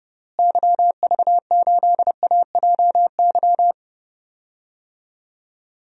Morse code, heard as a keyed tone, about 32 wpm, 700 Hz. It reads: YV8AJY
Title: DXCC-100 Call Signs - 6 Characters - Repeated 3x - 30wpm